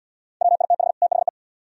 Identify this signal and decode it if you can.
Morse code, heard as a keyed tone, about 76 wpm, 700 Hz. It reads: WITH THE